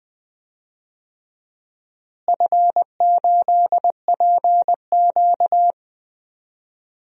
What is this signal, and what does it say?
Morse code, heard as a keyed tone, about 19 wpm, 700 Hz. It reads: F8PQ
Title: DXCC-100 Call Signs - 4 Characters - Repeated 3x - 20wpm